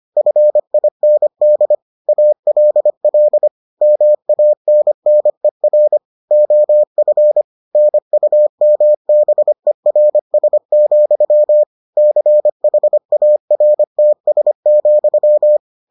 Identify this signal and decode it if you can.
Morse code, heard as a keyed tone, about 25 wpm, 600 Hz. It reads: FIND ALL MANNER OF NUMBERS, CHARTS,